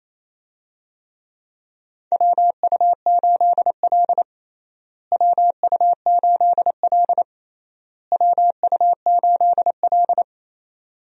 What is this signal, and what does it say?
Morse code, heard as a keyed tone, about 28 wpm, 700 Hz. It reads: WU8L WU8L WU8L
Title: USA Call Signs - 4 Characters - Repeated 3x - 28wpm